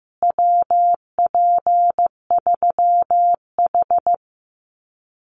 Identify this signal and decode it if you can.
Morse code, heard as a keyed tone, about 15 wpm, 700 Hz. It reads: WP3H
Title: Common QSO Phrases Encore 15wpm